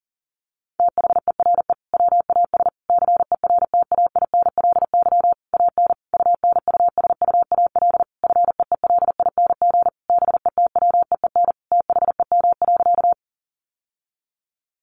Morse code, heard as a keyed tone, about 37 wpm, 700 Hz. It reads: THERE WAS CERTAINLY AN UNUSUAL FEELING BETWEEN THEM.
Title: Sentences from Top 1400 Words 40wpm